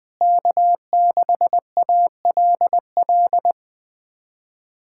Morse code, heard as a keyed tone, about 20 wpm, 700 Hz. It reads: K6ALL
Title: USA Callsign DE Callsign 20wpm